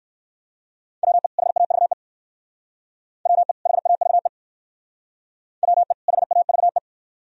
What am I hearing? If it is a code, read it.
Morse code, heard as a keyed tone, about 56 wpm, 690 Hz. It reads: WE HAVE WE HAVE WE HAVE